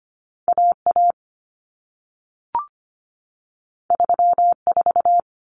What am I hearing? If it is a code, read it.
Morse code, heard as a keyed tone, about 25 wpm, 700 Hz. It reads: AA 34
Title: Common Cut Numbers - 2-Digit 25wpm